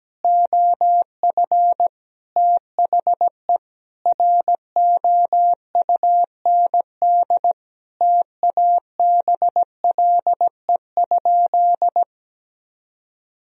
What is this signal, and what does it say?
Morse code, heard as a keyed tone, about 17 wpm, 700 Hz. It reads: OF THE ROUND TABLE?
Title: Sentences from Top 1000 Words - Review Part 2 - 17wpm